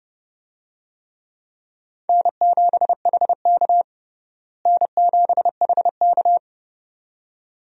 Morse code, heard as a keyed tone, about 32 wpm, 700 Hz. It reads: N7HK N7HK
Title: USA Call Signs - 4 Characters - Repeated 2x - 30wpm